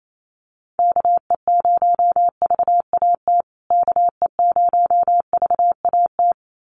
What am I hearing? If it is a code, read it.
Morse code, heard as a keyed tone, about 28 wpm, 700 Hz. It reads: KE0VAT KE0VAT
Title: USA Call Signs Repeated 2x 28wpm